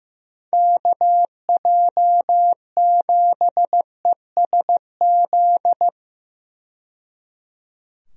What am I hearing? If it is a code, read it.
Morse code, heard as a keyed tone, about 15 wpm, 700 Hz. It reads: KJ7ESZ
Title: USA Call Signs - 6 Characters - 15wpm